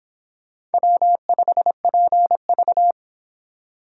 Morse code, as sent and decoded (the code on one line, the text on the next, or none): .-- ..... .--. ...-
W5PV